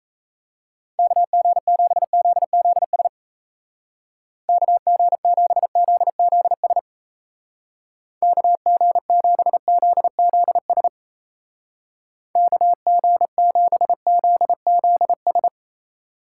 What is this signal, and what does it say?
Morse code, heard as a keyed tone, about 35 wpm, 700 Hz. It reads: KG7ZZS KG7ZZS KG7ZZS KG7ZZS
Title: Speed Racing - USA 2x3 Call Signs 28wpm